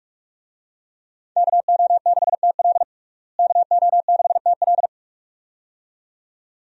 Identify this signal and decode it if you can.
Morse code, heard as a keyed tone, about 42 wpm, 700 Hz. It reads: KO6TL KO6TL
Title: USA Call Signs - 5 Characters - Repeated 2x - 45wpm